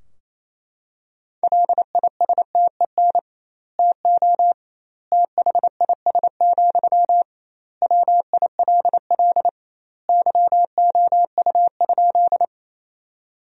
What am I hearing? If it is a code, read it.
Morse code, heard as a keyed tone, about 28 wpm, 700 Hz. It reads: LISTEN TO THIS, WILL YOU?
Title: 2x Word-Spacing - Sentences from Top 400 Words - 28wpm